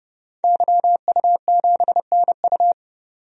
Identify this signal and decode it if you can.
Morse code, heard as a keyed tone, about 31 wpm, 700 Hz. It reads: YU7NU
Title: DXCC-100 Call Signs - 5 Characters - 30wpm